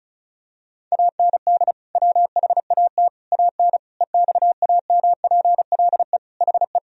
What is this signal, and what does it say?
Morse code, heard as a keyed tone, about 36 wpm, 700 Hz. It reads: AND WHAT AN EXAMPLE HE
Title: Sentences from Top 1000 Words - Review Part 1 - 35wpm